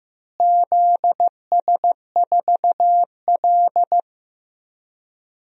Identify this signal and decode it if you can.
Morse code, heard as a keyed tone, about 15 wpm, 700 Hz. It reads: ZS4L
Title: DXCC-100 Call Signs - 4 Characters - Repeated 2x - 15wpm